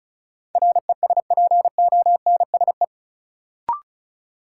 Morse code, heard as a keyed tone, about 34 wpm, 700 Hz. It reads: RESPONSE
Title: Sentences from Top 1400 Words - New Word Drill - 35wpm